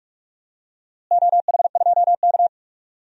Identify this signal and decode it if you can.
Morse code, heard as a keyed tone, about 41 wpm, 700 Hz. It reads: OH2K
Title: DXCC-100 Call Signs - 4 Characters - Repeated 2x - 45wpm